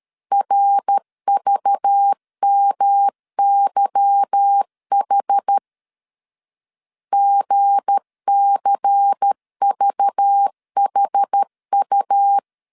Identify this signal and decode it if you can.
Morse code, heard as a keyed tone, about 13 wpm, 800 Hz. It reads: RVMYH GCVHU
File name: LEZIONI / 8 YJ45 / 8 Y 12wpm 2228